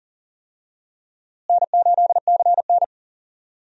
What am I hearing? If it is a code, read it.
Morse code, heard as a keyed tone, about 40 wpm, 700 Hz. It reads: N8CN
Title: USA Call Signs - 4 Characters - Repeated 2x - 40wpm